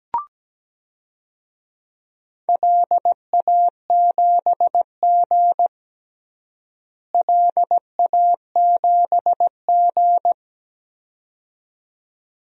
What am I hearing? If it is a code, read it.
Morse code, heard as a keyed tone, about 17 wpm, 700 Hz. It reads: LA7G LA7G
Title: DXCC-100 Call Signs - 4 Characters - Repeated 2x - 17wpm